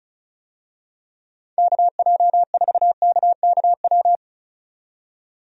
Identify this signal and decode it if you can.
Morse code, heard as a keyed tone, about 34 wpm, 700 Hz. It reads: KJ4KKW
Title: USA Call Signs - 6 Characters - 35wpm